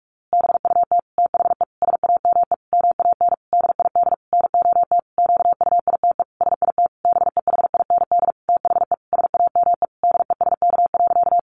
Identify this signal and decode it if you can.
Morse code, heard as a keyed tone, about 44 wpm, 700 Hz. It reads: BUT THE SAME MAN DID NOT QUITE SIT BEHIND THE SAME DESK.
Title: A Sound Of Thunder 45wpm